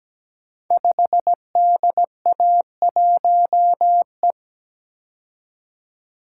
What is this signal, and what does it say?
Morse code, heard as a keyed tone, about 17 wpm, 700 Hz. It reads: 5DA1E
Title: Code Groups - 5 Letter:Numbers - 17wpm